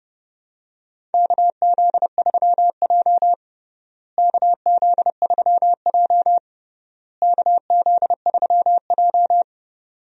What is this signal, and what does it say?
Morse code, heard as a keyed tone, about 30 wpm, 700 Hz. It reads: KZ3J KZ3J KZ3J
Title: USA Call Signs - 4 Characters - Repeated 3x - 30wpm